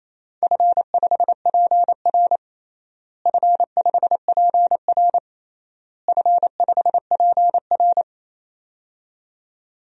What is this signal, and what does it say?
Morse code, heard as a keyed tone, about 28 wpm, 700 Hz. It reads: F5PR F5PR F5PR
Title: DXCC-100 Call Signs - 4 Characters - Repeated 3x - 28wpm